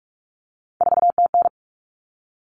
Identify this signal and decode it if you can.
Morse code, heard as a keyed tone, about 44 wpm, 700 Hz. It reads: 4TN
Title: All Cut Numbers - 3-Digit 45wpm